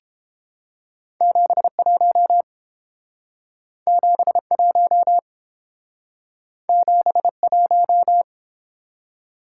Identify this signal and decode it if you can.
Morse code, heard as a keyed tone, about 29 wpm, 700 Hz. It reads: 71 71 71